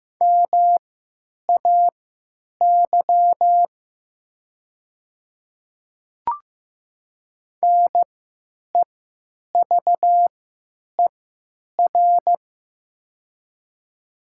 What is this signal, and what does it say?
Morse code, heard as a keyed tone, about 15 wpm, spaced as for 9 wpm, 690 Hz. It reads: MAY NEVER
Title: 3x Character Spacing - Top 300 Words - 15wpm